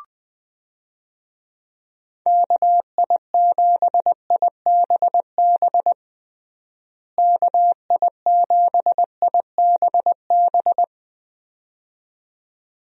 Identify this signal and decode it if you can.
Morse code, heard as a keyed tone, about 20 wpm, 700 Hz. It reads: KI7IBB KI7IBB
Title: USA Call Signs - 6 Characters - Repeated 2x - 20wpm